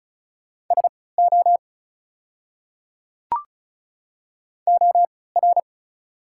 Morse code, heard as a keyed tone, about 36 wpm, spaced as for 20 wpm, 700 Hz. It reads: SO OR
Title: 3x Character Spacing - Top 100 Words - 35wpm